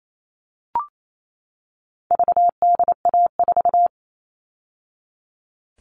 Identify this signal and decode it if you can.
Morse code, heard as a keyed tone, about 28 wpm, 700 Hz. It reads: VDA4
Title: All Cut Numbers - 4-Digit 28wpm